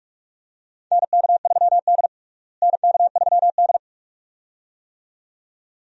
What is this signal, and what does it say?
Morse code, heard as a keyed tone, about 46 wpm, 700 Hz. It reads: NK3D NK3D
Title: USA Call Signs - 4 Characters - Repeated 2x - 45wpm